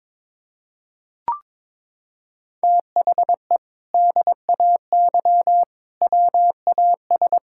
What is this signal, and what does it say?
Morse code, heard as a keyed tone, about 22 wpm, 700 Hz. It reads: THE DAY WAS
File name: practice / 22wpm / Sentences with 2-Word Phrases - 22wpm